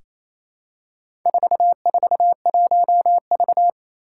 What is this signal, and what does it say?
Morse code, heard as a keyed tone, about 28 wpm, 700 Hz. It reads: 441V
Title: Four Challenging Letter-Number Combinations 28wpm